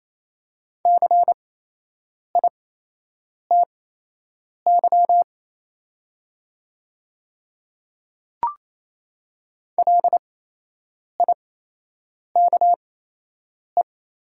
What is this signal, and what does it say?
Morse code, heard as a keyed tone, about 28 wpm, spaced as for 8 wpm, 690 Hz. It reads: CITY LIKE
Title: 8x Character Spacing - Top 500 Words - 28wpm